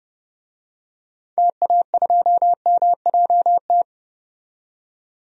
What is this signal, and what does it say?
Morse code, heard as a keyed tone, about 30 wpm, 690 Hz. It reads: TA2MJT
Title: DXCC-100 Call Signs - 6 Characters - 30wpm